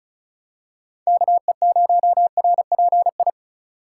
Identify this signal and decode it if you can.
Morse code, heard as a keyed tone, about 34 wpm, 700 Hz. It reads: KE0RPI
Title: USA Call Signs - 6 Characters - 4x Character Spacing - 35wpm